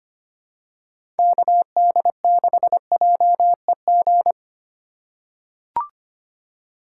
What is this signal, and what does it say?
Morse code, heard as a keyed tone, about 25 wpm, 700 Hz. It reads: KD6JEG
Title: USA Call Signs - 6 Characters - 4x Character Spacing - 25wpm